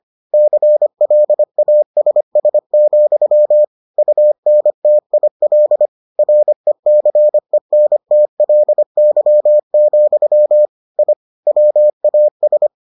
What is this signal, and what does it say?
Morse code, heard as a keyed tone, about 25 wpm, 600 Hz. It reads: CLASS, UNTIL RECENTLY, I WAS